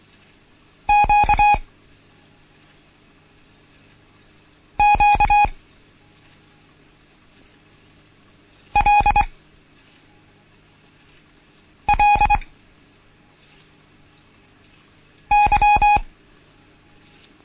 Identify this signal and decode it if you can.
Morse code, heard as a keyed tone, about 24 wpm, spaced as for 3 wpm, 830 Hz. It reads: QQLLY
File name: Morsen / Kurs A 16